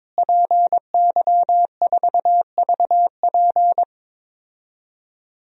Morse code, heard as a keyed tone, about 22 wpm, 700 Hz. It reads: PY4VP
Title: DXCC-100 Call Signs - 5 Characters - Repeated 3x - 22wpm